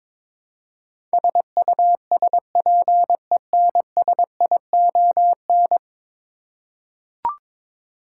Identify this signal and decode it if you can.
Morse code, heard as a keyed tone, about 23 wpm, 700 Hz. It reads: SUSPENSION
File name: practice / 22wpm / 10 Letter Words - First 3 Letters Given - 22wpm